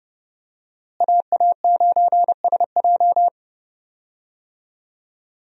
Morse code, heard as a keyed tone, about 30 wpm, 700 Hz. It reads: AA9SJ
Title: USA Call Signs - 5 Characters - Repeated 3x - 30wpm